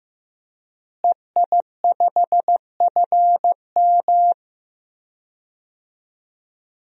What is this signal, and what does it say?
Morse code, heard as a keyed tone, about 15 wpm, 700 Hz. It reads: EI5FM
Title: DXCC-100 Call Signs - 5 Characters - Repeated 2x - 15wpm